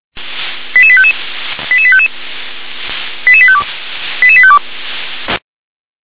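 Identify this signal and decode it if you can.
Five-tone selective call (ZVEI), heard as quick runs of five tones, tones of about 70 ms. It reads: 88855 88855 88852 88852
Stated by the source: ZVEI five-tone selective call received on SDR.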